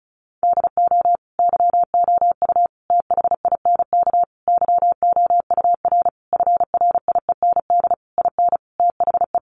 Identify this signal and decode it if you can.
Morse code, heard as a keyed tone, about 34 wpm, 700 Hz. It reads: DO YOU THINK YOUR FRIEND IN THE
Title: Sentences from Top 1100 Words 35wpm